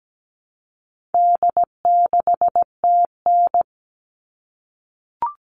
Morse code, heard as a keyed tone, about 17 wpm, 700 Hz. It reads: D6TN
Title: All Cut Numbers - 4-Digit 17wpm